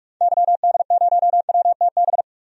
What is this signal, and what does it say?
Morse code, heard as a keyed tone, about 41 wpm, 700 Hz. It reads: YD0WTB
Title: DXCC-100 Call Signs - 6 Characters - 45wpm